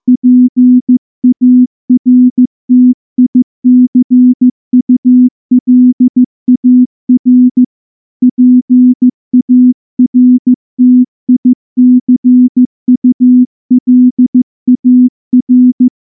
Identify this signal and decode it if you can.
Morse code, heard as a keyed tone, about 15 wpm, 250 Hz. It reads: PARTICULAR PARTICULAR
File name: morse / morse-training-15WPM